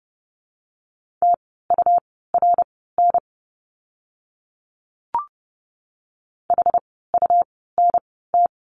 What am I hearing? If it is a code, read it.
Morse code, heard as a keyed tone, about 30 wpm, spaced as for 17 wpm, 700 Hz. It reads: TURN HUNT
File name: Lesson 17f - Letter u - Words 30wpm